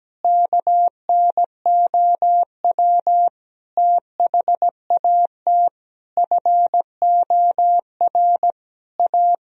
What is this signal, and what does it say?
Morse code, heard as a keyed tone, about 17 wpm, 700 Hz. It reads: KNOW THAT FOR A
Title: Sentences from Top 1000 Words - Review Part 3 - 17wpm